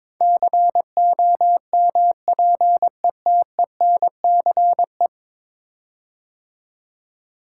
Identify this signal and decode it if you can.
Morse code, heard as a keyed tone, about 22 wpm, 700 Hz. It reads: COMPETENCE
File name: practice / 22wpm / 10 Letter Words - First 3 Letters Given - 22wpm